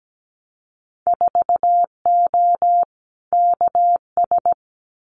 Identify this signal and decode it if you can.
Morse code, heard as a keyed tone, about 17 wpm, 700 Hz. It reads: 4O KS
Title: Winter Field Day Exchanges Encore 17wpm